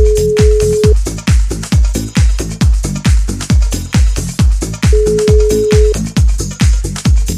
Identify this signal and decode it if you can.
Ringback tone (Europe, 425 Hz).